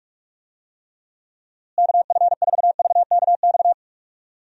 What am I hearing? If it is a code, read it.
Morse code, heard as a keyed tone, about 43 wpm, 700 Hz. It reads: KF4VKX
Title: USA Call Signs - 6 Characters - Repeated 2x - 45wpm